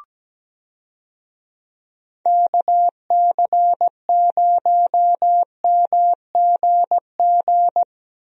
Morse code, heard as a keyed tone, about 17 wpm, 700 Hz. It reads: KC0MGG
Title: USA Call Signs - 6 Characters - Repeated 2x - 17wpm